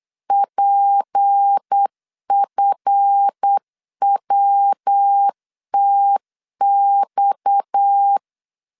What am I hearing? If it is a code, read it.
Morse code, heard as a keyed tone, about 8 wpm, 800 Hz. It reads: PFWTX
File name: LEZIONI / 8 YJ45 / 8 YJ45 8wpm 3214